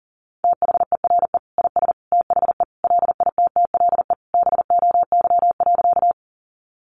Morse code, heard as a keyed tone, about 40 wpm, 700 Hz. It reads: THERE IS THE LITTLE BOY.
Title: Sentences from Top 200 Words 40wpm